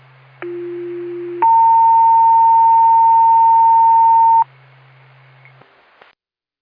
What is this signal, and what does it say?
Two-tone page, heard as a one-second tone, then a three-second tone, about 350 then 910 Hz. It is heard from 0.4 to 4.4 s.